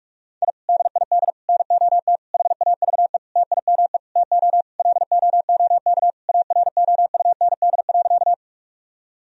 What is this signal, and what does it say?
Morse code, heard as a keyed tone, about 46 wpm, 700 Hz. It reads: I DID NOT HAVE TIME TO LOOK AROUND.
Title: Sentences from Top 1000 Words - Review Part 3 - 45wpm